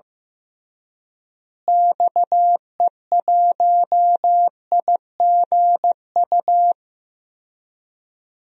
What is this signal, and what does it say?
Morse code, heard as a keyed tone, about 15 wpm, 700 Hz. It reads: XE1IGU
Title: DXCC-100 Call Signs - 6 Characters - Repeated 3x - 15wpm